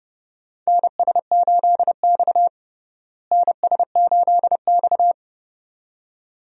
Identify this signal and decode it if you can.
Morse code, heard as a keyed tone, about 30 wpm, 700 Hz. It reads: NS8X NS8X
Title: USA Call Signs - 4 Characters - Repeated 2x - 30wpm